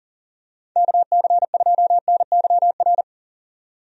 Morse code, heard as a keyed tone, about 40 wpm, 700 Hz. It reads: KC2NYR
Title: USA Call Signs - 6 Characters - 40wpm